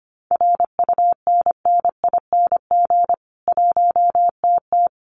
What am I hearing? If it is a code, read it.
Morse code, heard as a keyed tone, about 25 wpm, 700 Hz. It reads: RUNNING 1TT
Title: Common QSO Phrases 25wpm